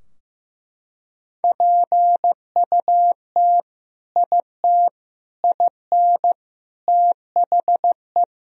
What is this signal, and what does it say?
Morse code, heard as a keyed tone, about 16 wpm, 700 Hz. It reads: PUT IT IN THE
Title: Sentences from Top 1000 Words - Review Part 4 - 15wpm